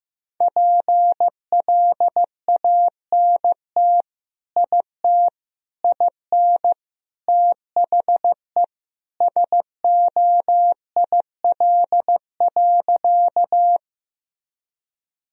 Morse code, heard as a keyed tone, about 15 wpm, 700 Hz. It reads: PLANT IT IN THE SOIL.
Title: Sentences from Top 1000 Words - Review Part 4 - 15wpm